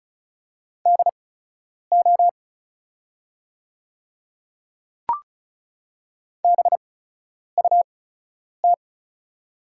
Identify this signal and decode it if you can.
Morse code, heard as a keyed tone, about 34 wpm, spaced as for 10 wpm, 700 Hz. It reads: DO BUT
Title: 8x Character Spacing - Top 100 Words - 35wpm